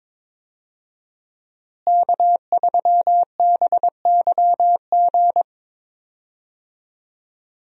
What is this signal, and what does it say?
Morse code, heard as a keyed tone, about 22 wpm, 700 Hz. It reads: K3BYG
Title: USA Call Signs - 5 Characters - Repeated 3x - 22wpm